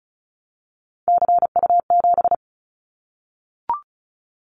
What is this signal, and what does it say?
Morse code, heard as a keyed tone, about 36 wpm, 690 Hz. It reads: CU7
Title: Callsign Prefixes for Top 50 DXCC - 35wpm